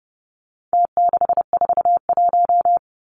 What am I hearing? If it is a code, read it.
Morse code, heard as a keyed tone, about 30 wpm, 700 Hz. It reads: T641
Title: Numbers - Contest Style - 4-Digit 30wpm